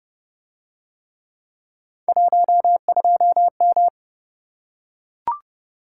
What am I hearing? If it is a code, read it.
Morse code, heard as a keyed tone, about 29 wpm, 700 Hz. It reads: 12M